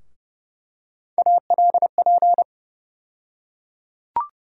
Morse code, heard as a keyed tone, about 30 wpm, 700 Hz. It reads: ALP
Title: USA Call Sign Postfixes - 3 Letters 30wpm